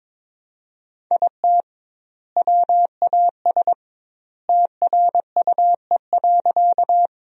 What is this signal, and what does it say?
Morse code, heard as a keyed tone, about 22 wpm, 700 Hz. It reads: IT WAS TRUE.
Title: 2x Word-Spacing - Sentences from Top 400 Words - 22wpm